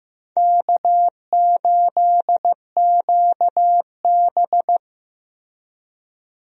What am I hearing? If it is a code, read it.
Morse code, heard as a keyed tone, about 15 wpm, 700 Hz. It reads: K8QB
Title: USA Call Signs - 4 Characters - Repeated 3x - 15wpm